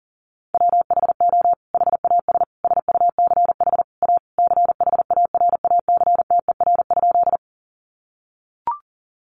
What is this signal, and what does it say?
Morse code, heard as a keyed tone, about 40 wpm, 700 Hz. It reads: WHO HAS SUCH A CHARACTER?
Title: Sentences from Top 1200 Words 40wpm